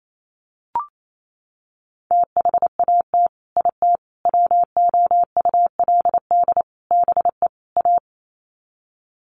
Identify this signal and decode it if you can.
Morse code, heard as a keyed tone, about 28 wpm, 700 Hz. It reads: THAT IT WOULD BE A
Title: Sets of 5 Words Encore2 28wpm